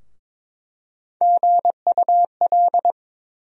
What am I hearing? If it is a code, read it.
Morse code, heard as a keyed tone, about 22 wpm, 700 Hz. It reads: GUL
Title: USA Call Sign Postfixes - 3 Letters 22wpm